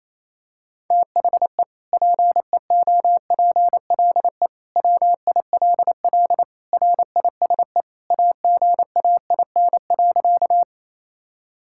Morse code, heard as a keyed tone, about 28 wpm, 700 Hz. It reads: THE PEOPLE WILL RISE AGAIN.
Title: Sentences from Top 1000 Words - Review Part 6 - 28wpm